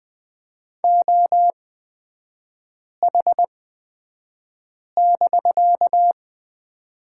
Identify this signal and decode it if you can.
Morse code, heard as a keyed tone, about 20 wpm, spaced as for 5 wpm, 700 Hz. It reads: OH<BK>
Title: 250 Lesson 40e - BK - Three Characters 20wpm